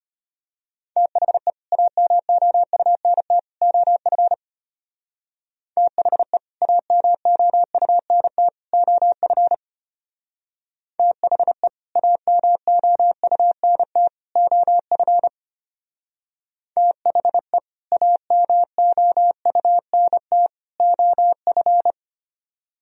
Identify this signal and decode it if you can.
Morse code, heard as a keyed tone, about 31 wpm, 700 Hz. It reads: THE AMOUNT OF THE AMOUNT OF THE AMOUNT OF THE AMOUNT OF
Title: Speed-Racing - Sets of 3 Words 25wpm